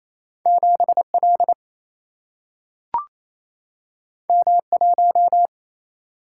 Morse code, heard as a keyed tone, about 28 wpm, 700 Hz. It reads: 7 L M 1
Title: Single-Letter Single-Number 28wpm